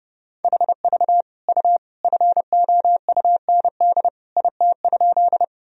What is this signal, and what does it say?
Morse code, heard as a keyed tone, about 32 wpm, 700 Hz. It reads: HV U FOUND IT?
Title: Sentences from Top 300 Words and Abbreviations 30wpm